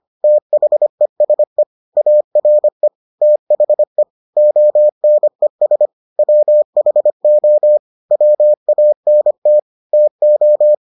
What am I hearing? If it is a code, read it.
Morse code, heard as a keyed tone, about 25 wpm, 600 Hz. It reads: THESE ARE THE ONES WHO WANT TO